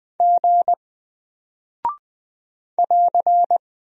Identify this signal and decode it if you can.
Morse code, heard as a keyed tone, about 20 wpm, 700 Hz. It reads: G <AR>
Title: Single Character (Letter:Number:Punctuation:Prosign) Alt 20wpm